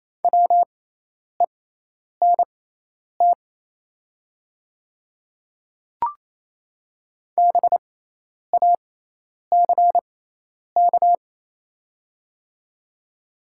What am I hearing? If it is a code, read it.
Morse code, heard as a keyed tone, about 28 wpm, spaced as for 10 wpm, 690 Hz. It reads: WENT BACK
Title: 6x Character Spacing - Top 300 Words - 28wpm